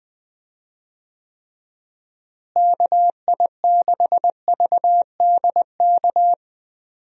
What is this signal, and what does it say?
Morse code, heard as a keyed tone, about 20 wpm, 700 Hz. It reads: KI6VDK
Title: USA Call Signs - 6 Characters - Repeated 2x - 20wpm